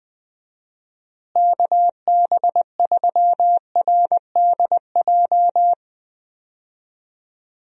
Morse code, heard as a keyed tone, about 20 wpm, 700 Hz. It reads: KB3RDJ
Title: USA Call Signs - 6 Characters - Repeated 2x - 20wpm